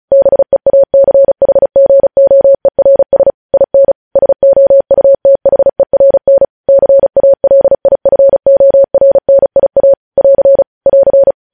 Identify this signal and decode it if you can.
Morse code, heard as a keyed tone, about 36 wpm, 550 Hz. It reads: BEACHGOERS IN SOUTHERN CALIFORNIA <AR> <AR>
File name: podcast / News headlines-1x-35x35